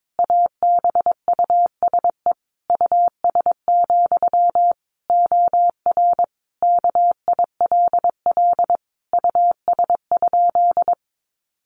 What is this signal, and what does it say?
Morse code, heard as a keyed tone, about 22 wpm, 700 Hz. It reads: ABUSE US, OR KILL US?
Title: Sentences from Top 1400 Words 22wpm